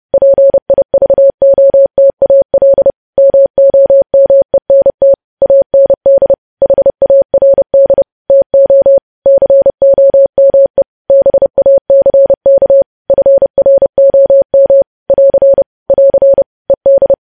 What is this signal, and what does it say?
Morse code, heard as a keyed tone, about 30 wpm, 550 Hz. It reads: PIVOTAL MOMENT AND HARD TO COME BACK FROM <AR> <AR> ED
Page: News Headlines in Morse Code at 30 WPM